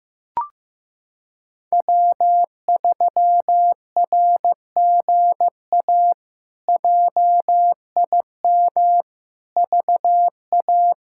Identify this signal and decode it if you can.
Morse code, heard as a keyed tone, about 16 wpm, 700 Hz. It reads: W3RGA JIM VA
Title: K1USN SST QSO Practice - 15wpm